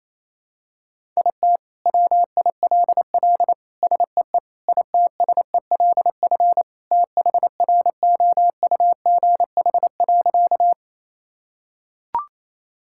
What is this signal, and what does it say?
Morse code, heard as a keyed tone, about 28 wpm, 700 Hz. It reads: IT WILL SEE ITSELF THROUGH.
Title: Sentences from Top 1000 Words - Review Part 5 - 28wpm